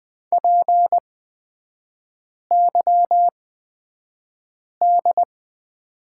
Morse code, heard as a keyed tone, about 20 wpm, spaced as for 5 wpm, 700 Hz. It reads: PYD
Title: 243 Lesson 39e - Letter z - Three Characters 20wpm